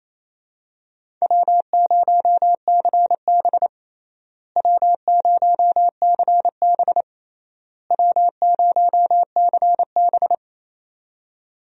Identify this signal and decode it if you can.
Morse code, heard as a keyed tone, about 28 wpm, 700 Hz. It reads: W0CB W0CB W0CB
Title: USA Call Signs - 4 Characters - Repeated 3x - 28wpm